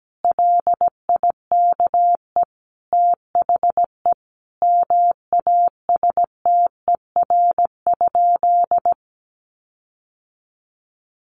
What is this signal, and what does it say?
Morse code, heard as a keyed tone, about 17 wpm, 700 Hz. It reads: LIKE THE MASTER?
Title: Sentences from Top 1300 Words 17wpm